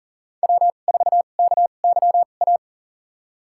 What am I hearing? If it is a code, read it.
Morse code, heard as a keyed tone, about 40 wpm, spaced as for 29 wpm, 700 Hz. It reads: W4KYA
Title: USA Call Signs - 2x Character Spacing - 40wpm